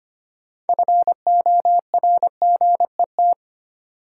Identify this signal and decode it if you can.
Morse code, heard as a keyed tone, about 25 wpm, 700 Hz. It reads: FORGET
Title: Sentences from Top 1200 Words - New Word Drill - 25wpm